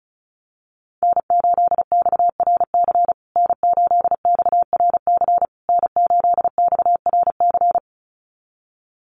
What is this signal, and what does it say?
Morse code, heard as a keyed tone, about 34 wpm, 700 Hz. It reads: N8XRC N8XRC N8XRC
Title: USA Call Signs Repeated 3x 35wpm